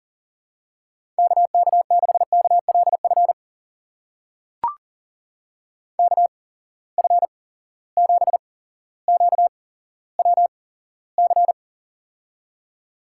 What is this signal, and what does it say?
Morse code, heard as a keyed tone, about 39 wpm, spaced as for 17 wpm, 700 Hz. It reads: KK6KLF KF7QWC
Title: USA Call Signs - 6 Characters - 8x Character Spacing - 40wpm